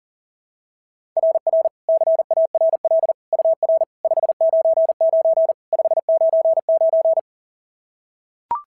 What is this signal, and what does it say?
Morse code, heard as a keyed tone, about 40 wpm, 690 Hz. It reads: RR CARL UR 599 599
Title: SOTA QSO Practice 40wpm